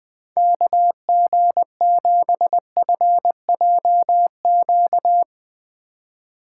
Morse code, heard as a keyed tone, about 20 wpm, 700 Hz. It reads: KG7FJQ
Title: Speed Racing - USA Call Signs 17wpm